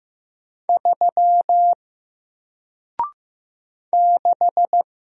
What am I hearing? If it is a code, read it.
Morse code, heard as a keyed tone, about 15 wpm, 700 Hz. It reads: 3 6
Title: Single Character (Letter:Number:Punctuation:Prosign) Alt 15wpm